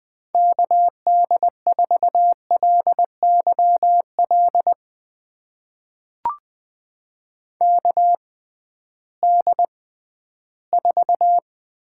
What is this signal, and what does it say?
Morse code, heard as a keyed tone, about 21 wpm, spaced as for 13 wpm, 700 Hz. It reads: KD4LYL KD4
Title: USA Call Signs - 6 Characters - 6x Character Spacing - 20wpm